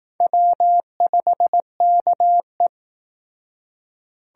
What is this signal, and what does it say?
Morse code, heard as a keyed tone, about 19 wpm, 700 Hz. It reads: W5KE